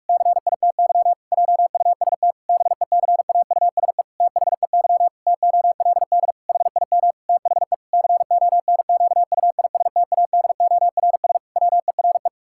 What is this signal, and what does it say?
Morse code, heard as a keyed tone, about 42 wpm, 700 Hz. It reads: CITY JUST BECAUSE THEY TOLD HIM THE CONQUISTADORS WERE
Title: Pangram Challenge v2 - 45wpm